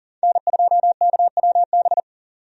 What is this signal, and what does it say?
Morse code, heard as a keyed tone, about 40 wpm, 700 Hz. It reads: N2KWB
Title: USA Call Signs - 5 Characters - 40wpm